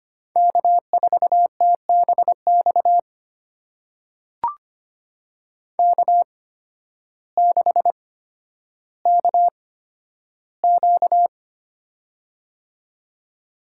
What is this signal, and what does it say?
Morse code, heard as a keyed tone, about 25 wpm, spaced as for 12 wpm, 700 Hz. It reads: K4TBX K6KQ
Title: USA Call Signs - 8x Character Spacing - 25wpm